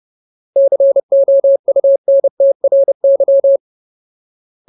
Morse code, heard as a keyed tone, about 30 wpm, 550 Hz. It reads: COUNTRY
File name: Ultimate Word List / 30WPM Ultimate